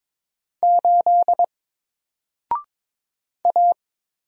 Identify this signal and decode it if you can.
Morse code, heard as a keyed tone, about 22 wpm, 700 Hz. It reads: 8 A
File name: Single-Letter-Number-22wpm